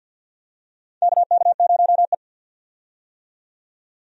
Morse code, heard as a keyed tone, about 49 wpm, 700 Hz. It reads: KK0E
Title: USA Call Signs - 4 Characters - 50wpm